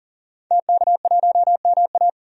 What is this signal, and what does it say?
Morse code, heard as a keyed tone, about 40 wpm, 700 Hz. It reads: TK1MA
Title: DXCC-100 Call Signs - 5 Characters - 40wpm